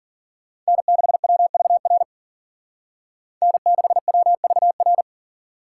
Morse code, heard as a keyed tone, about 52 wpm, 700 Hz. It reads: N6WVR N6WVR